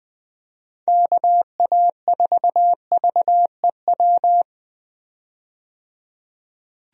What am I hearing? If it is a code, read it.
Morse code, heard as a keyed tone, about 21 wpm, 700 Hz. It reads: KA4VEW
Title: USA Call Signs - 6 Characters - 20wpm